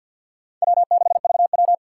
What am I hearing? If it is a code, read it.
Morse code, heard as a keyed tone, about 50 wpm, 700 Hz. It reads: W6VW
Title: USA Call Signs - 2x Character Spacing - 50wpm